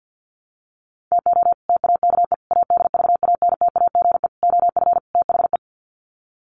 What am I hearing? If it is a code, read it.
Morse code, heard as a keyed tone, about 50 wpm, 700 Hz. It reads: TO TAKE ADVANTAGE OF THE
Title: Sets of 5 Words 50wpm